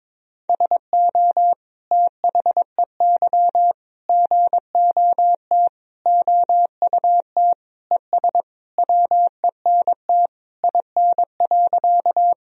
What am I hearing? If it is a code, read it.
Morse code, heard as a keyed tone, about 22 wpm, 700 Hz. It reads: SO THEY GOT OUT ES WENT IN.
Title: Sentences from Top 300 Words and Abbreviations 22wpm